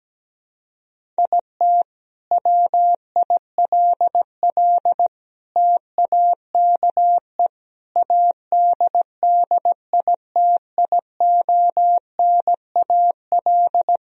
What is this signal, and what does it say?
Morse code, heard as a keyed tone, about 17 wpm, 700 Hz. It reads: IT WILL TAKE ADDITIONAL
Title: Sentences from Top 1000 Words - Review Part 4 - 17wpm